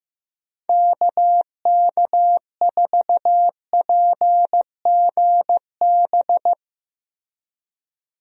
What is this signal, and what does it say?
Morse code, heard as a keyed tone, about 15 wpm, 700 Hz. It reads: KK4PGB
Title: USA Call Signs - 6 Characters - Repeated 2x - 15wpm